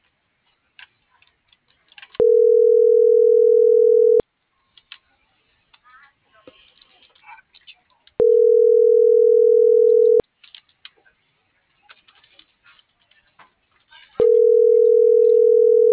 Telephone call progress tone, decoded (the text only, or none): ringback tone